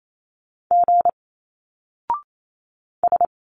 Morse code, heard as a keyed tone, about 27 wpm, 700 Hz. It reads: G S
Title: Single Letters 28wpm